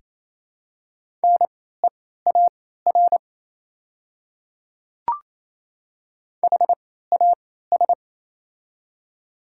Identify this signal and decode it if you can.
Morse code, heard as a keyed tone, about 28 wpm, spaced as for 16 wpm, 690 Hz. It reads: NEAR HAS
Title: 3x Character Spacing - Top 200 Words - 28wpm